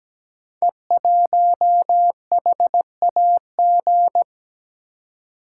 Morse code, heard as a keyed tone, about 17 wpm, 700 Hz. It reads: E1HAG
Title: Code Groups - 5 Letter:Numbers - 17wpm